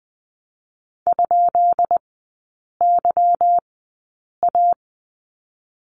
Morse code, heard as a keyed tone, about 20 wpm, spaced as for 8 wpm, 700 Hz. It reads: ?YA
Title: 145 Lesson 24e - Letter y - Three Characters 20wpm